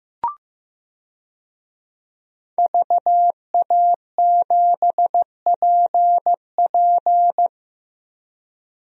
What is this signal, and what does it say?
Morse code, heard as a keyed tone, about 16 wpm, 700 Hz. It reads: VA7PP
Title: DXCC-100 Call Signs - 5 Characters - Repeated 3x - 15wpm